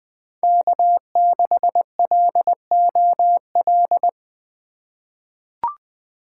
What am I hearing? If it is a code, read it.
Morse code, heard as a keyed tone, about 20 wpm, 700 Hz. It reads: K6LOL
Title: USA Call Signs - 5 Characters - 3x Character Spacing - 20wpm